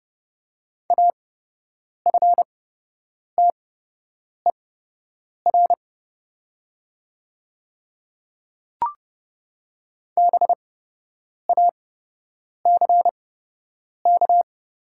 Morse code, heard as a keyed tone, about 30 wpm, spaced as for 8 wpm, 700 Hz. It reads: AFTER BACK
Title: 8x Character Spacing - Top 300 Words - 30wpm